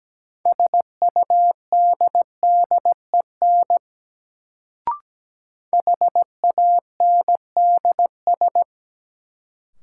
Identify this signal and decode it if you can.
Morse code, heard as a keyed tone, about 17 wpm, 700 Hz. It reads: SUDDEN HANDS
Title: Sets of 4 Words - New Word Drill - 17wpm